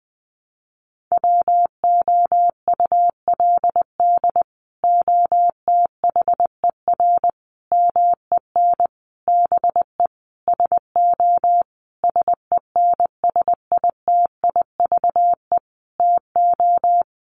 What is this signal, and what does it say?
Morse code, heard as a keyed tone, about 20 wpm, 700 Hz. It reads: WOULD OTHER MEN BE SO SENSITIVE TO
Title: Sentences from Top 1400 Words 20wpm